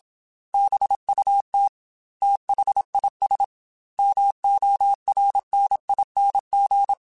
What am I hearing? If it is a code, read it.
Morse code, heard as a keyed tone, about 26 wpm, 800 Hz. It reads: BUT THIS MORNING
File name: Weather Sets / Weather Sets-mp3s / Weather 3 17 @25